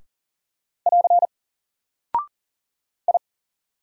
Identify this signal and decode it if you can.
Morse code, heard as a keyed tone, about 39 wpm, 700 Hz. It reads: <AR> I
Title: Single Character (Letter:Number:Punctuation:Prosign) 40wpm